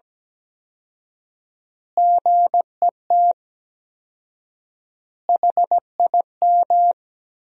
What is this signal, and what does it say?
Morse code, heard as a keyed tone, about 17 wpm, 700 Hz. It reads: GET HIM
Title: 4x Word-Spacing - Sentences from Top 400 Words - 17wpm